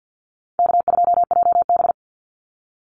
Morse code, heard as a keyed tone, about 50 wpm, 700 Hz. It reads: K2JB
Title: NA SOTA Activator Callsigns 50wpm